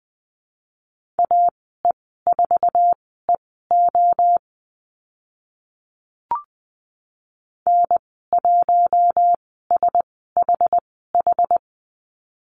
Morse code, heard as a keyed tone, about 20 wpm, spaced as for 14 wpm, 700 Hz. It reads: AE4EO N1SHH
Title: 062 Lesson 11g - Letter h - Call Signs 20wpm